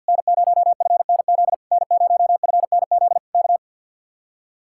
Morse code, heard as a keyed tone, about 67 wpm, spaced as for 58 wpm, 700 Hz. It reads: N0FNZ N0FNZ K
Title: Speed-Racing Common QSO Phrases - 50wpm